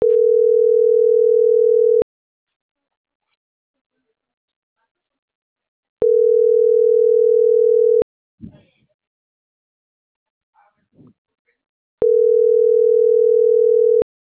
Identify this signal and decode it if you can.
Ringback tone (North America), 440 + 480 Hz.